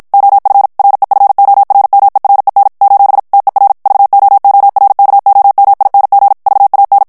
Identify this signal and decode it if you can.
Morse code, heard as a keyed tone, about 51 wpm, 800 Hz. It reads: OF REFORMEREN : TER VOORKOMING VAN
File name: morse mp3 / HOLLANDAIS texte clair 2 vitesse 50 mots minute